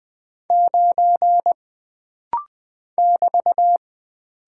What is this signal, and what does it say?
Morse code, heard as a keyed tone, about 20 wpm, 700 Hz. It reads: 9 <BT>
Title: Single Character (Letter:Number:Punctuation:Prosign) - Rapid-Fire - 20wpm